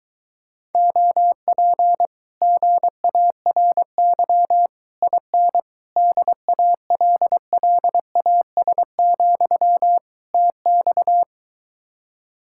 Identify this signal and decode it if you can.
Morse code, heard as a keyed tone, about 22 wpm, 700 Hz. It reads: OP GARY IN DALLAS, TX